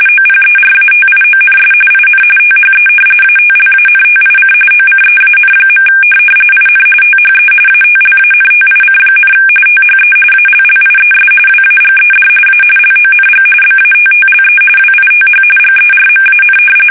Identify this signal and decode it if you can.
RTTY, about 45.45 baud. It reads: OPERATIONS IN THE ATLANTIC OCEAN. IN THE ATLANTIC, THE ALLIES WAGED A LONG BATTLE WITH GERMAN
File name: fsk850